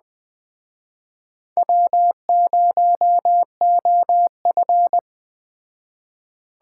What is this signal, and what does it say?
Morse code, heard as a keyed tone, about 22 wpm, 700 Hz. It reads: W0OF